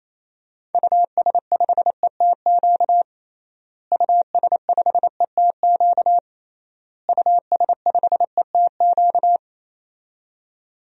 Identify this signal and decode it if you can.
Morse code, heard as a keyed tone, about 28 wpm, 700 Hz. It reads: US5ETQ US5ETQ US5ETQ
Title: DXCC-100 Call Signs - 6 Characters - Repeated 3x - 28wpm